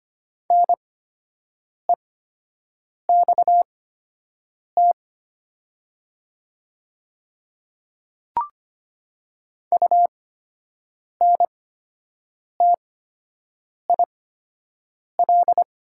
Morse code, heard as a keyed tone, about 25 wpm, spaced as for 7 wpm, 700 Hz. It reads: NEXT UNTIL
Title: 8x Character Spacing - Top 300 Words - 25wpm